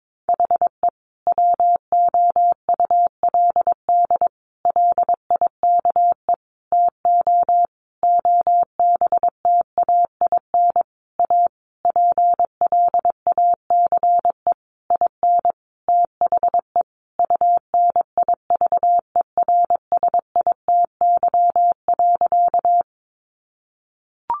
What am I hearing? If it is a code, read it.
Morse code, heard as a keyed tone, about 22 wpm, 700 Hz. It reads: HE WOULD LIKE TO OBTAIN A PLACE IN THE UNIVERSITY.
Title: Sentences from Top 1300 Words 22wpm